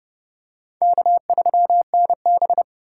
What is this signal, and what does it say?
Morse code, heard as a keyed tone, about 31 wpm, 700 Hz. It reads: K3NB
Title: USA Call Signs - 4 Characters - 4x Character Spacing - 30wpm